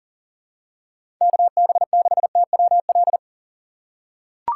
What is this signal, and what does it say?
Morse code, heard as a keyed tone, about 40 wpm, 700 Hz. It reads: KB6TWL
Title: USA Call Signs - 6 Characters - 2x Character Spacing - 40wpm